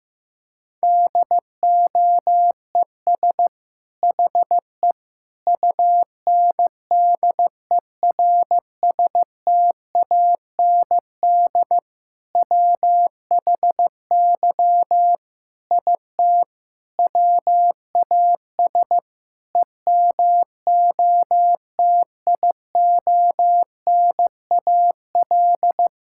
Morse code, heard as a keyed tone, about 15 wpm, 690 Hz. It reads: DOES HE UNDERSTAND WHY IT WAS EMOTIONAL
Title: Sentences from Top 1000 Words - Review Part 5 - 15wpm